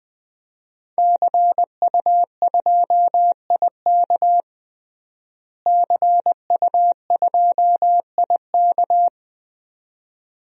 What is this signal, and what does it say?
Morse code, heard as a keyed tone, about 21 wpm, 700 Hz. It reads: CU2IK CU2IK
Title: DXCC-100 Call Signs - 5 Characters - Repeated 2x - 20wpm